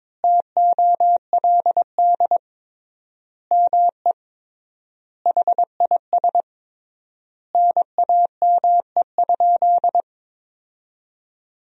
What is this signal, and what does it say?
Morse code, heard as a keyed tone, about 22 wpm, 700 Hz. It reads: TOLD ME HIS NAME?
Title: 3x Word-Spacing - Sentences from Top 400 Words - 22wpm